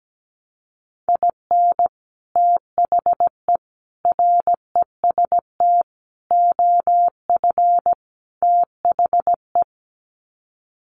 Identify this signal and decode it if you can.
Morse code, heard as a keyed tone, about 17 wpm, 700 Hz. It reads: IN THE REST OF THE
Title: Sets of 5 Words 17wpm